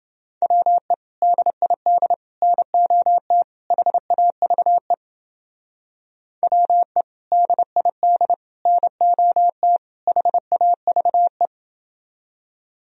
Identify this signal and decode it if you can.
Morse code, heard as a keyed tone, about 28 wpm, 700 Hz. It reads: WE DID NOT HAVE WE DID NOT HAVE